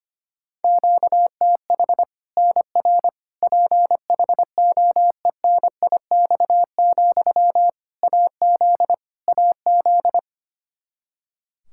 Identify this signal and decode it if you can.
Morse code, heard as a keyed tone, about 25 wpm, 700 Hz. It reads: QTH NR PHOENIX, AZ AZ
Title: Practice QSO - Exchange 1 - Beginning - 25wpm